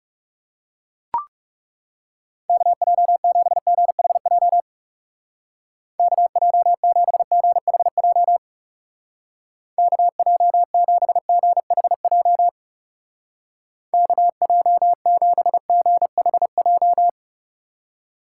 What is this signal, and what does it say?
Morse code, heard as a keyed tone, about 37 wpm, 700 Hz. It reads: KJ7GHJ KJ7GHJ KJ7GHJ KJ7GHJ
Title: Speed Racing - USA 2x3 Call Signs 30wpm